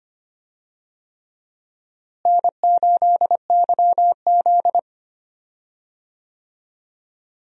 Morse code, heard as a keyed tone, about 25 wpm, 700 Hz. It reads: N8YZ
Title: USA Call Signs - 4 Characters - Repeated 3x - 25wpm